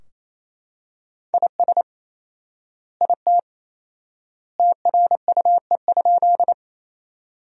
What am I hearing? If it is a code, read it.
Morse code, heard as a keyed tone, about 28 wpm, 700 Hz. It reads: IS IT TRUE?
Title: 4x Word-Spacing - Sentences from Top 400 Words - 28wpm